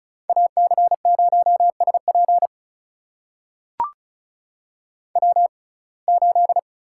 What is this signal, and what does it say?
Morse code, heard as a keyed tone, about 34 wpm, spaced as for 25 wpm, 700 Hz. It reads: AC0SP W8
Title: USA Call Signs - 5 Characters - 6x Character Spacing - 35wpm